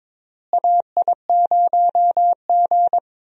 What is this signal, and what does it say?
Morse code, heard as a keyed tone, about 22 wpm, 700 Hz. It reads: AI0G
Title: USA Call Signs - 4 Characters - Repeated 3x - 22wpm